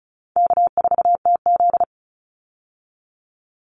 Morse code, heard as a keyed tone, about 34 wpm, 700 Hz. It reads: K4TZ
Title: Easy Call Signs Repeated 2x - 35wpm